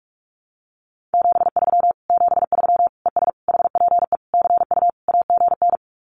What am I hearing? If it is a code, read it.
Morse code, heard as a keyed tone, about 46 wpm, 700 Hz. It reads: 73 73 ES HPE CU AGN
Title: Common QSO Phrases Encore 45wpm